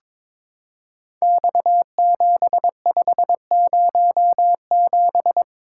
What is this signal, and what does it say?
Morse code, heard as a keyed tone, about 23 wpm, 700 Hz. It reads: X7507
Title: Code Groups - 5 Letter:Numbers - 22wpm